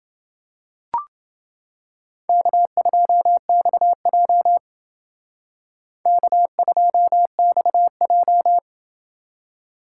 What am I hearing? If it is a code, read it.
Morse code, heard as a keyed tone, about 28 wpm, 700 Hz. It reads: K2XJ K2XJ